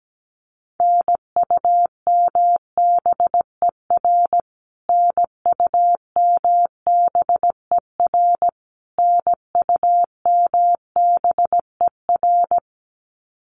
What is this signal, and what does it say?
Morse code, heard as a keyed tone, about 17 wpm, 700 Hz. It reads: NUMBER NUMBER NUMBER
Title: Top 500 Words Repeated 17wpm